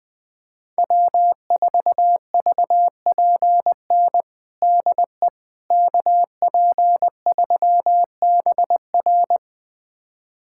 Morse code, heard as a keyed tone, about 21 wpm, 700 Hz. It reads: W4VPN DE KP3BR
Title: USA Callsign DE Callsign - 5 Characters - 20wpm